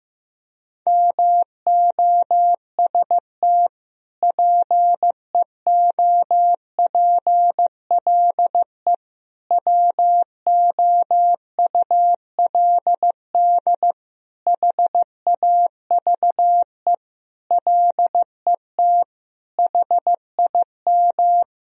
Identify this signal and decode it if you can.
Morse code, heard as a keyed tone, about 15 wpm, 700 Hz. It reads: MOST PEOPLE WOULD HAVE LET HIM
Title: Sentences from Top 1000 Words - Review Part 1 - 15wpm